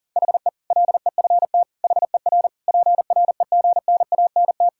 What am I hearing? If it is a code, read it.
Morse code, heard as a keyed tone, about 37 wpm, 700 Hz. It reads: HE LEFT HER PREGNANT
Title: Sentences from Top 1000 Words - Review Part 6 - 40wpm